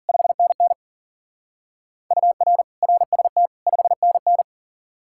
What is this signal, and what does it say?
Morse code, heard as a keyed tone, about 49 wpm, 700 Hz. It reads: 5NN UR RST 5NN
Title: Speed-Racing Common QSO Phrases - 40wpm